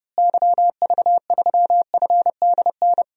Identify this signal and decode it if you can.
Morse code, heard as a keyed tone, about 30 wpm, 700 Hz. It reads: YV3FDN
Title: DXCC-100 Call Signs - 6 Characters - Repeated 3x - 30wpm